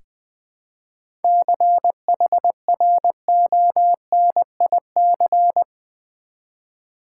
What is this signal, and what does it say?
Morse code, heard as a keyed tone, about 20 wpm, 700 Hz. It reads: CHRONIC
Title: 7 Letter Words - First 3 Letters Given - 20wpm